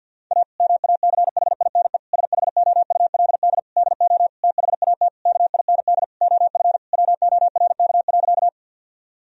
Morse code, heard as a keyed tone, about 50 wpm, 700 Hz. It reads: A MACHINE SHOULD DO THAT KIND OF WORK.
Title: Sentences from Top 1000 Words - Review Part 2 - 50wpm